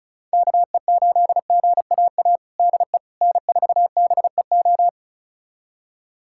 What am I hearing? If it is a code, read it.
Morse code, heard as a keyed tone, about 36 wpm, 700 Hz. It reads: KE8GAA DE N4BEO
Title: USA Callsign DE Callsign 35wpm